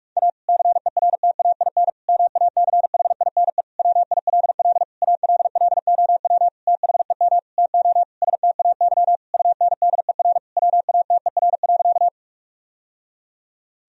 Morse code, heard as a keyed tone, about 46 wpm, 700 Hz. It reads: A CERTAIN MACHINE WILL ALLOW THEM TO STAY UNDER WATER.
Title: Sentences from Top 1000 Words - Review Part 2 - 45wpm